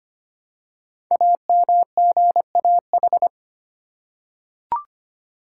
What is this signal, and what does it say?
Morse code, heard as a keyed tone, about 25 wpm, 700 Hz. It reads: AMGAH
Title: Code Groups - 5 Letters - 25wpm